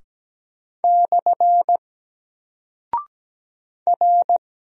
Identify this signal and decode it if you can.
Morse code, heard as a keyed tone, about 17 wpm, 700 Hz. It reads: / R
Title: Single Character (Letter:Number:Punctuation:Prosign) Alt 17wpm